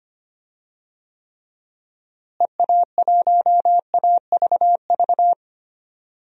Morse code, heard as a keyed tone, about 25 wpm, 700 Hz. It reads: EA1AVV
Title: DXCC-100 Call Signs - 6 Characters - Repeated 3x - 25wpm